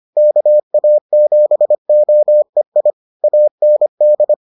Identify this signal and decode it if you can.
Morse code, heard as a keyed tone, about 25 wpm, 600 Hz. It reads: KA7OEI AND